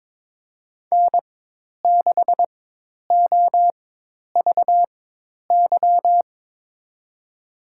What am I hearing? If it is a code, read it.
Morse code, heard as a keyed tone, about 21 wpm, spaced as for 10 wpm, 700 Hz. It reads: N6OVY
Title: USA Call Signs - 5 Characters - 4x Character Spacing - 22wpm